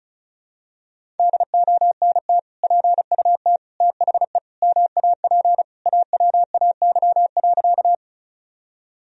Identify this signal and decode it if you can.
Morse code, heard as a keyed tone, about 36 wpm, 700 Hz. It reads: DONT PUT THE MAP AWAY.
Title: Sentences from Top 1000 Words - Review Part 2 - 35wpm